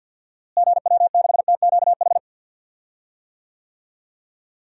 Morse code, heard as a keyed tone, about 46 wpm, 700 Hz. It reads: GW6TQH
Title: DXCC-100 Call Signs - 6 Characters - Repeated 2x - 50wpm